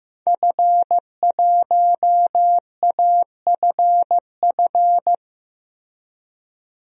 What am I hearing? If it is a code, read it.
Morse code, heard as a keyed tone, about 15 wpm, 700 Hz. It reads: F1AFF
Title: DXCC-100 Call Signs - 5 Characters - 15wpm